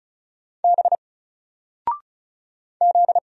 Morse code, heard as a keyed tone, about 34 wpm, 700 Hz. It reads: B Z
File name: Single-Letter-Number-35wpm